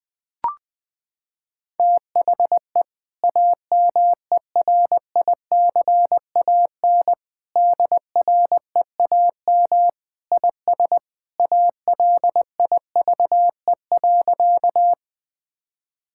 Morse code, heard as a keyed tone, about 20 wpm, 700 Hz. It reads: THE AMERICAN DREAM IS ALIVE.
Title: Sentences with 2-Word Phrases - 20wpm